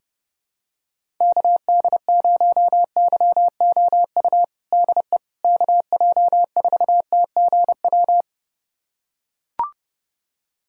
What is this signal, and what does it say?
Morse code, heard as a keyed tone, about 32 wpm, 700 Hz. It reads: KD0YOU DE KJ4TGW
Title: USA Callsign DE Callsign - 6 Characters - 30wpm